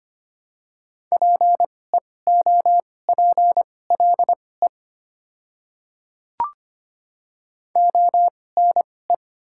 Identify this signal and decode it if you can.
Morse code, heard as a keyed tone, about 25 wpm, spaced as for 18 wpm, 700 Hz. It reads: PEOPLE ONE
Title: 2x Character Spacing - Top 100 Words - 25wpm